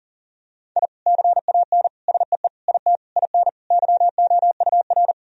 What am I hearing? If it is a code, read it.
Morse code, heard as a keyed tone, about 37 wpm, 700 Hz. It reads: I CAN SEE IT IN YOUR
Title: Sentences from Top 1000 Words - Review Part 2 - 40wpm